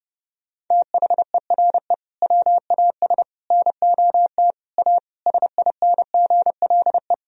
Morse code, heard as a keyed tone, about 30 wpm, 700 Hz. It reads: THERE WAS NOT A SINGLE
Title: Sentences from Top 1000 Words - Review Part 4 - 30wpm